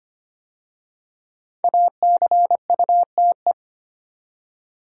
Morse code, heard as a keyed tone, about 25 wpm, 700 Hz. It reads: ACUTE
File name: 5 Letter Words - First 3 Letters Given - 25wpm